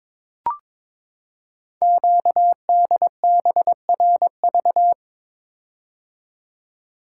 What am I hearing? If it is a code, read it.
Morse code, heard as a keyed tone, about 22 wpm, 700 Hz. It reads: QDBRV
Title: Code Groups - 5 Letters - 22wpm